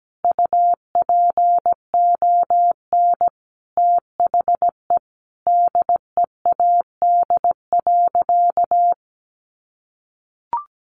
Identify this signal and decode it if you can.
Morse code, heard as a keyed tone, about 17 wpm, 700 Hz. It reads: UPON THE DEAD.
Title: Sentences from Top 1500 Words 17wpm